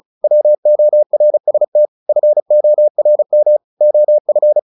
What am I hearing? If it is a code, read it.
Morse code, heard as a keyed tone, about 34 wpm, 600 Hz. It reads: WORST FORM OF
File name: the old man and the sea / the old man and the sea 1 35wpm